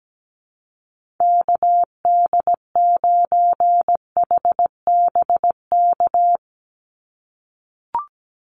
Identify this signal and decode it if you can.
Morse code, heard as a keyed tone, about 17 wpm, 700 Hz. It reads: KD9HBK
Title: USA Call Signs 17wpm